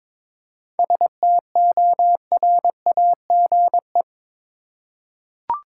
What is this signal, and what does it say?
Morse code, heard as a keyed tone, about 21 wpm, 700 Hz. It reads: STORAGE
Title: Seven Letter Words - 22wpm